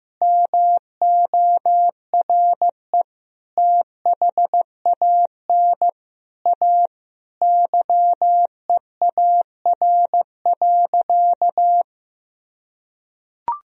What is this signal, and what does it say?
Morse code, heard as a keyed tone, about 15 wpm, 700 Hz. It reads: MORE THAN A YEAR.
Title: Sentences from Top 1000 Words - Review Part 1 - 15wpm